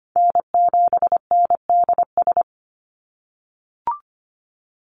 Morse code, heard as a keyed tone, about 25 wpm, 700 Hz. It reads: N7NDS
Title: USA Call Signs 25wpm